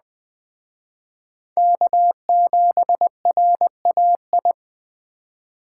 Morse code, heard as a keyed tone, about 20 wpm, 700 Hz. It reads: K7RAI